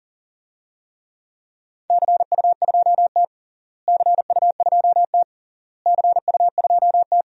Morse code, heard as a keyed tone, about 37 wpm, 700 Hz. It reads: CU2T CU2T CU2T
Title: DXCC-100 Call Signs - 4 Characters - Repeated 3x - 40wpm